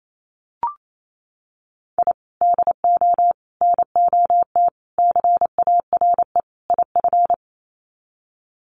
Morse code, heard as a keyed tone, about 28 wpm, 700 Hz. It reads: I DO NOT CARE IF
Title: Sets of 5 Words 28wpm